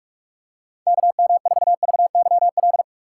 Morse code, heard as a keyed tone, about 46 wpm, 690 Hz. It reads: KM4VYL
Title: USA Call Signs - 4x Character Spacing - 45wpm